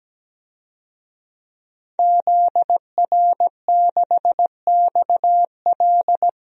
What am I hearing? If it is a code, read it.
Morse code, heard as a keyed tone, about 17 wpm, 700 Hz. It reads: ZR6XL
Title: DXCC-100 Call Signs - 5 Characters - Repeated 2x - 17wpm